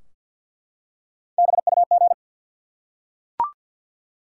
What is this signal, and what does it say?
Morse code, heard as a keyed tone, about 52 wpm, 700 Hz. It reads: BUG
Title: Three Letter Words - 50wpm